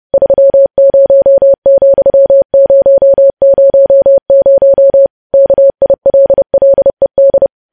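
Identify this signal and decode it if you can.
Morse code, heard as a keyed tone, about 30 wpm, 550 Hz. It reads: 30,000 KILLED
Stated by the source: Morse news headlines sent at 30 wpm.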